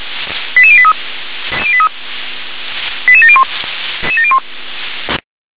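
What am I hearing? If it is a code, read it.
Five-tone selective call (ZVEI), heard as quick runs of five tones, tones of about 70 ms. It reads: 88083 88701
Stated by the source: Five-tone ZVEI selective call received by SDR.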